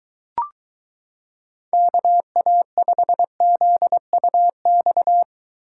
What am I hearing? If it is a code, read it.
Morse code, heard as a keyed tone, about 23 wpm, 700 Hz. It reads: KA5ZUX